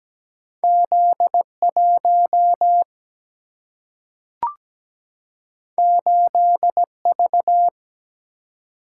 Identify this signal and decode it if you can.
Morse code, heard as a keyed tone, about 17 wpm, 700 Hz. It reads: Z1 8V
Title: Single-Letter Single-Number 17wpm